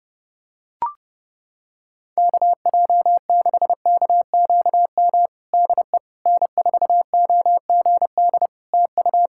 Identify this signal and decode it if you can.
Morse code, heard as a keyed tone, about 32 wpm, 700 Hz. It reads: KJ6KQM DE N4OGD TU
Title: Practice QSO - Exchange 4 - Ending - 30wpm